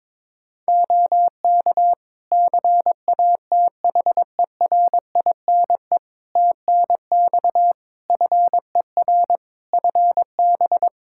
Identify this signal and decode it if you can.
Morse code, heard as a keyed tone, about 22 wpm, 700 Hz. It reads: OK CATHERINE TNX FER FB
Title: Practice QSO - Exchange 4 - Ending - 22wpm